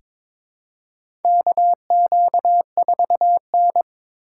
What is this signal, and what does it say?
Morse code, heard as a keyed tone, about 21 wpm, 700 Hz. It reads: KQ4N
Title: USA Call Signs - 4 Characters - 2x Character Spacing - 22wpm